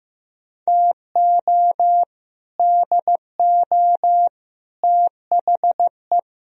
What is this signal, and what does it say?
Morse code, heard as a keyed tone, about 16 wpm, 700 Hz. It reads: TO DO THE
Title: Speed-Racing - Sets of 5 Words 15wpm